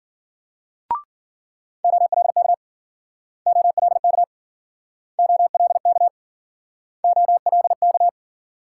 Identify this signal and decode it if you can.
Morse code, heard as a keyed tone, about 51 wpm, 700 Hz. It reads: OLK OLK OLK OLK